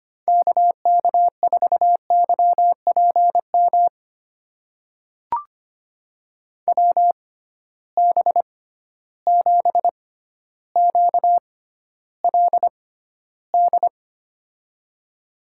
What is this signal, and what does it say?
Morse code, heard as a keyed tone, about 25 wpm, spaced as for 13 wpm, 700 Hz. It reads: KK4YPM WB7QLD
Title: USA Call Signs - 6 Characters - 6x Character Spacing - 25wpm